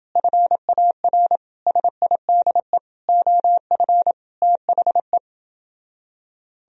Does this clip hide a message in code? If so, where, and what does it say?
Yes: Morse code from 0.2 to 5.2 s, heard as a keyed tone, about 29 wpm, 700 Hz. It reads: FAR SIDE OF THE